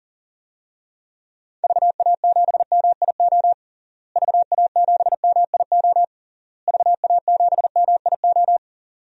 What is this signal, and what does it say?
Morse code, heard as a keyed tone, about 40 wpm, 700 Hz. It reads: VA7MIO VA7MIO VA7MIO
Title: DXCC-100 Call Signs - 6 Characters - Repeated 3x - 40wpm